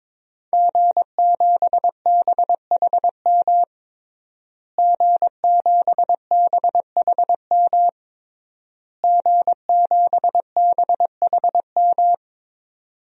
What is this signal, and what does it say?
Morse code, heard as a keyed tone, about 22 wpm, 700 Hz. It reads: G7BHM G7BHM G7BHM
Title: DXCC-100 Call Signs - 5 Characters - Repeated 3x - 22wpm